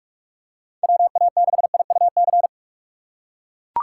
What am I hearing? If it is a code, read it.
Morse code, heard as a keyed tone, about 41 wpm, 700 Hz. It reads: WA6IUC
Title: USA Call Signs - 6 Characters - 6x Character Spacing - 45wpm